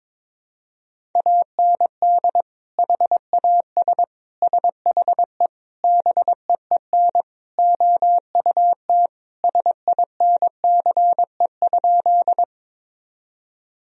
Morse code, heard as a keyed tone, about 22 wpm, 700 Hz. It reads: AND HAS SHE BEEN OUT SINCE?
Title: Sentences from Top 1000 Words - Review Part 1 - 22wpm